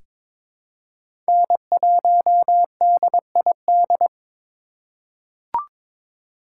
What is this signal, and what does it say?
Morse code, heard as a keyed tone, about 22 wpm, 700 Hz. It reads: N1DID
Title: K1USN SST QSO Practice - 22wpm